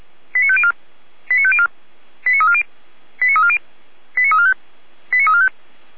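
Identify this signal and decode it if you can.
Five-tone selective call (ZVEI), heard as quick runs of five tones, tones of about 70 ms. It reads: 89504 89504 89340 89340 89345 89345
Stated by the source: ZVEI five-tone call received by SDR, from the feuerwehr-orlamuende.org fire brigade alerting audio archive.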